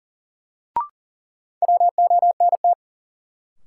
Morse code, heard as a keyed tone, about 40 wpm, 700 Hz. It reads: WONT
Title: Sentences from Top 1000 Words - Review Part 6 - New Word Drill - 40wpm